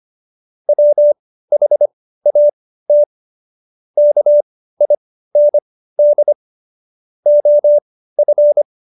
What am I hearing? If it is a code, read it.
Morse code, heard as a keyed tone, about 25 wpm, spaced as for 15 wpm, 590 Hz. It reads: WHAT KIND OF
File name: Jokes-25-char-15-eff